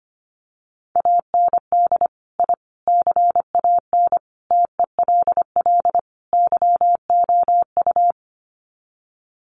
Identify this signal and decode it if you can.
Morse code, heard as a keyed tone, about 25 wpm, 700 Hz. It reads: AND I CAN TELL YOU
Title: Sets of 5 Words Encore2 25wpm